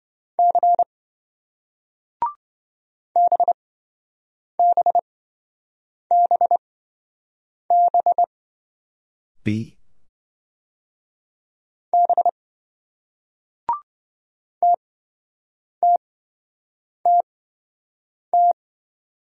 Morse code, heard as a keyed tone, about 26 wpm, spaced as for 7 wpm, 700 Hz. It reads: C BBBB B TTTT